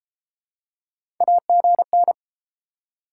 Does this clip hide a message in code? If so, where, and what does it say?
Yes: Morse code from 1.2 to 2.1 s, heard as a keyed tone, about 30 wpm, 700 Hz. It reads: AGN